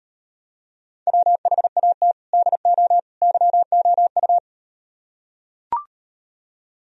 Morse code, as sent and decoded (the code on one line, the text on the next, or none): .-- .... .- - / -.. --- / -.-- --- ..-
WHAT DO YOU